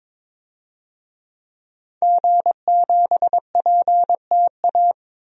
Morse code, heard as a keyed tone, about 22 wpm, 700 Hz. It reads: G7PTA
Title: DXCC-100 Call Signs - 5 Characters - 22wpm